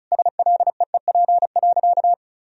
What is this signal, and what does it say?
Morse code, heard as a keyed tone, about 36 wpm, 700 Hz. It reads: SLEEP.
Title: Sentences from Top 1000 Words - Review Part 2 - 35wpm